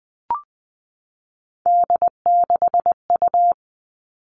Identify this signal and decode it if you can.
Morse code, heard as a keyed tone, about 20 wpm, 700 Hz. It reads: D6U
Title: All Cut Numbers - 3-Digit 20wpm